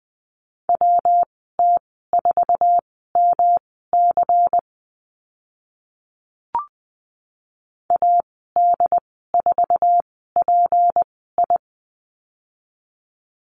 Morse code, heard as a keyed tone, about 20 wpm, spaced as for 15 wpm, 700 Hz. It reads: WT4MC AD4PI
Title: 193 Lesson 31g - Letter v - Call Signs 20wpm